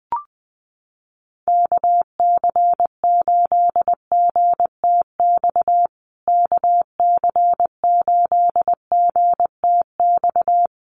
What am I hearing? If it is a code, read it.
Morse code, heard as a keyed tone, about 20 wpm, 700 Hz. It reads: KC8GTX KC8GTX
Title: USA Call Signs Repeated 2x 20wpm